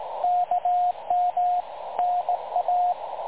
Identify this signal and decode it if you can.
Morse code, heard as a keyed tone, about 18 wpm, 720 Hz, with radio noise. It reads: KM NA
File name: dlazaawansowanych / nagraniazpasma / websdr recording 3565.0kHz